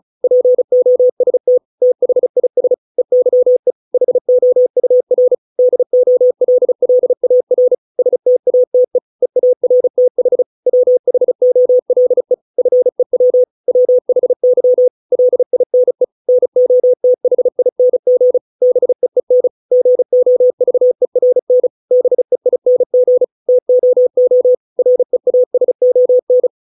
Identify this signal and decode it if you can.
Morse code, heard as a keyed tone, about 34 wpm, 500 Hz. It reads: POST THIS EYE HOUR DOLLAR STATE EARTH WHOLE FEW WHY LINE NOTHING BEEN GOVERN BEING TOO REASON